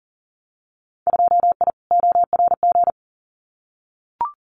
Morse code, heard as a keyed tone, about 39 wpm, 700 Hz. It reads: 2I ORG
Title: Winter Field Day Exchanges Encore 40wpm